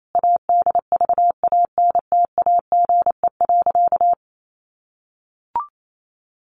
Morse code, heard as a keyed tone, about 28 wpm, 700 Hz. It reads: ADVANTAGE.
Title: Sentences from Top 900 Words 28wpm